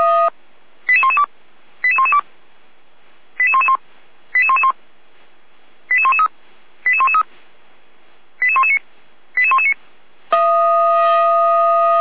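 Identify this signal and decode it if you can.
Five-tone selective call (ZVEI), heard as quick runs of five tones, tones of about 80 ms. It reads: 88112 88112 88111 88111 88113 88113 88119 88119
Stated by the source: ZVEI five-tone selective call received on SDR by feuerwehr-orlamuende.org, fire service.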